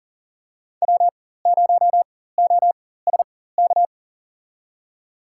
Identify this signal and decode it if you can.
Morse code, heard as a keyed tone, about 37 wpm, spaced as for 18 wpm, 700 Hz. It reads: W0OSK
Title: USA Call Signs - 4x Character Spacing - 40wpm